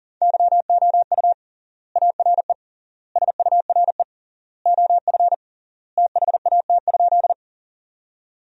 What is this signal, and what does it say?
Morse code, heard as a keyed tone, about 40 wpm, 700 Hz. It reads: YOU ARE SURE OF THAT?
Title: 3x Word-Spacing - Sentences from Top 300 Words - 40wpm